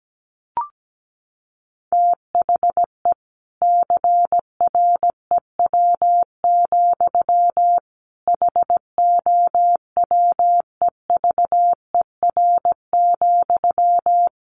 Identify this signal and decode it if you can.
Morse code, heard as a keyed tone, about 17 wpm, 700 Hz. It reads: THE CREW, HOWEVER,
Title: Sentences from Top 1500 Words 17wpm